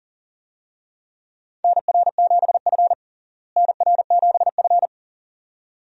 Morse code, heard as a keyed tone, about 40 wpm, 700 Hz. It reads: NR7F NR7F
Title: USA Call Signs - 4 Characters - Repeated 2x - 40wpm